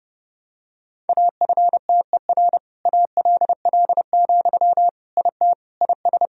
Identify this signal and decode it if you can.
Morse code, heard as a keyed tone, about 28 wpm, 700 Hz. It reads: AFTER ALL, IT IS
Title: Sentences from Top 1000 Words - Review Part 4 - 30wpm